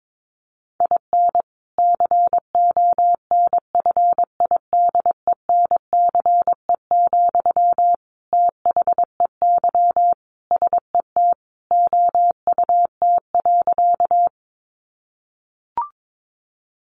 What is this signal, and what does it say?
Morse code, heard as a keyed tone, about 22 wpm, 700 Hz. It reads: IN CONFIDENCE, THEY SET OUT.
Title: Sentences from Top 1500 Words 22wpm